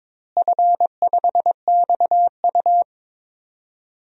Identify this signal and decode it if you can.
Morse code, heard as a keyed tone, about 22 wpm, 700 Hz. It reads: F5XU
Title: DXCC-100 Call Signs - 4 Characters - Repeated 2x - 22wpm